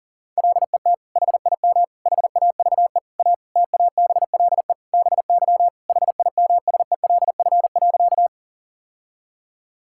Morse code, heard as a keyed tone, about 40 wpm, 700 Hz. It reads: LET HIM HAVE A TABLE BY HIMSELF.
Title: Sentences from Top 1600 Words 40wpm